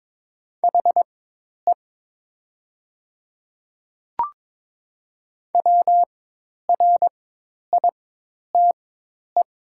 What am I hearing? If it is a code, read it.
Morse code, heard as a keyed tone, about 22 wpm, spaced as for 10 wpm, 700 Hz. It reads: HE WRITE
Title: 4x Character Spacing - Top 200 Words - 22wpm